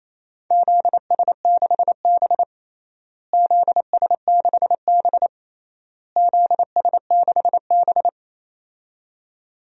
Morse code, heard as a keyed tone, about 28 wpm, 700 Hz. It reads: ZS6B ZS6B ZS6B
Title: DXCC-100 Call Signs - 4 Characters - Repeated 3x - 28wpm